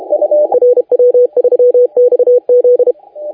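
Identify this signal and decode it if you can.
Morse code, heard as a keyed tone, about 32 wpm, 490 Hz, with radio noise. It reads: RW3XZ
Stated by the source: Morse contest audio recorded by LA4C during the SAC CW 2018 contest.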